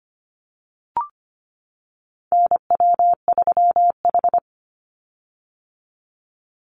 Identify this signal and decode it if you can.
Morse code, heard as a keyed tone, about 25 wpm, 700 Hz. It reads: NW3H
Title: USA Call Signs 25wpm